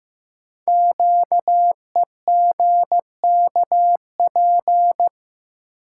Morse code, heard as a keyed tone, about 15 wpm, 700 Hz. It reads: QEGKP
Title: Code Groups - 5 Letters - 15wpm